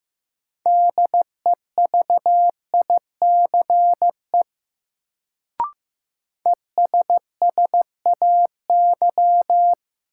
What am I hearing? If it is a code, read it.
Morse code, heard as a keyed tone, about 15 wpm, 700 Hz. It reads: DEVICE ESSAY
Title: Sentences from Top 1500 Words - New Word Drill Alt - 15wpm